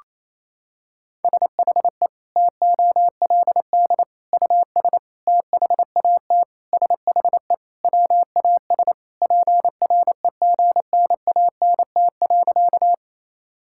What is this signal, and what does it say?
Morse code, heard as a keyed tone, about 28 wpm, 700 Hz. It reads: SHE TOLD US THAT SHE WAS PREGNANT.
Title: Sentences from Top 1000 Words - Review Part 6 - 28wpm